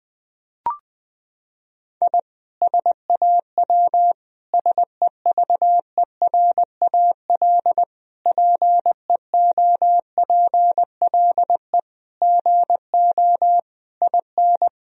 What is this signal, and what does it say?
Morse code, heard as a keyed tone, about 21 wpm, 700 Hz. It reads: I SAW SEVERAL PEOPLE GO IN
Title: Sentences from Top 1000 Words - Review Part 2 - 20wpm